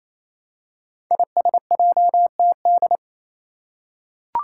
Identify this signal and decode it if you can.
Morse code, heard as a keyed tone, about 28 wpm, 700 Hz. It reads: ISJTD
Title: Five Challenging Letter-Number Combinations 28wpm